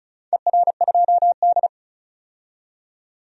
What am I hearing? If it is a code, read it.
Morse code, heard as a keyed tone, about 36 wpm, 700 Hz. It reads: ER2D
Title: DXCC-100 Call Signs - 4 Characters - 35wpm